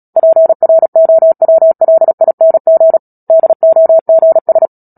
Morse code, heard as a keyed tone, about 37 wpm, 650 Hz. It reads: PROWLING DOGS
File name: Book Two / 10-Chapter 10 - Epilogue - 36 WPM